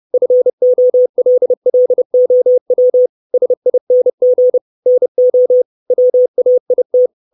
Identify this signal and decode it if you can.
Morse code, heard as a keyed tone, about 30 wpm, 500 Hz. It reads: FOLLOW SING NO WAIT
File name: 632 Words 30WPM